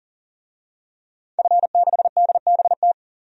Morse code, heard as a keyed tone, about 40 wpm, 700 Hz. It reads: F6DBT
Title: DXCC-100 Call Signs - 5 Characters - 40wpm